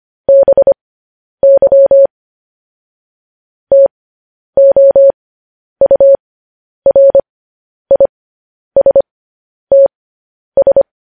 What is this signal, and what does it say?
Morse code, heard as a keyed tone, about 25 wpm, spaced as for 10 wpm, 550 Hz. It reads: BY TOURISTS
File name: podcast / News headlines-1x-25x10